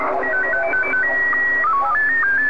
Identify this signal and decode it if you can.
SSTV (slow-scan television), Scottie 1.